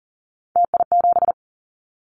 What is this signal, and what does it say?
Morse code, heard as a keyed tone, about 40 wpm, 700 Hz. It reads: TI7
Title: Callsign Prefixes for Top 100 DXCC - 40wpm